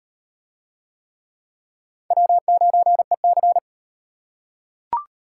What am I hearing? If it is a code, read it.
Morse code, heard as a keyed tone, about 36 wpm, 700 Hz. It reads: W9EC